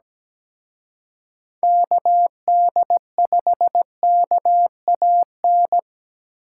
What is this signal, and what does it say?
Morse code, heard as a keyed tone, about 20 wpm, 700 Hz. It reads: KD5KAN
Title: Speed Racing - USA Call Signs 17wpm